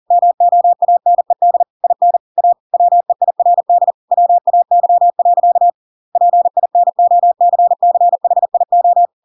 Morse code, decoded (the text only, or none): MOANED IN A WEIRD WAY. PINOCCHIO